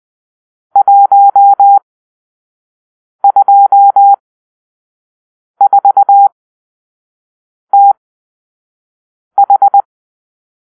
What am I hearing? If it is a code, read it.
Morse code, heard as a keyed tone, about 20 wpm, spaced as for 5 wpm, 800 Hz. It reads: 124TH
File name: arrl / Grp-6 Drill-3